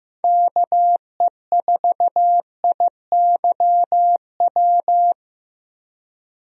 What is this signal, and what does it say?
Morse code, heard as a keyed tone, about 15 wpm, 700 Hz. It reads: KE4IYW
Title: USA Call Signs - 6 Characters - 15wpm